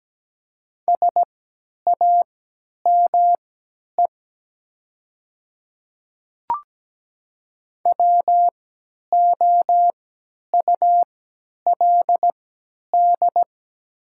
Morse code, heard as a keyed tone, about 17 wpm, spaced as for 10 wpm, 700 Hz. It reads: SAME WOULD
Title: 3x Character Spacing - Top 200 Words - 17wpm